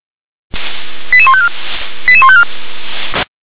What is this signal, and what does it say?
Five-tone selective call (ZVEI), heard as quick runs of five tones, tones of about 70 ms. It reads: 88154 88154
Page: /357/SDR/ZVEI/88646/2021.07.21/